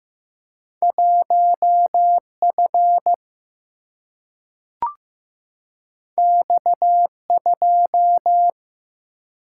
Morse code, heard as a keyed tone, about 15 wpm, 700 Hz. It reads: 1F X2
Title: Single-Letter Single-Number 15wpm